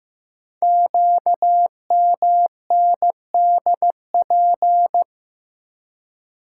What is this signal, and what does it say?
Morse code, heard as a keyed tone, about 15 wpm, 700 Hz. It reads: QMNDP
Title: Code Groups - 5 Letters - 15wpm